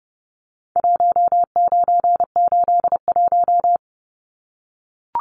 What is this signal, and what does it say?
Morse code, heard as a keyed tone, about 30 wpm, 700 Hz. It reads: 1981
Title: Numbers - Contest Style - 4-Digit 30wpm